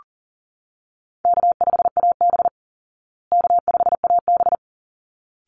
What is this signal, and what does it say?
Morse code, heard as a keyed tone, about 40 wpm, 700 Hz. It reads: K5AB K5AB
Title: Call Signs Repeated 3x - 40wpm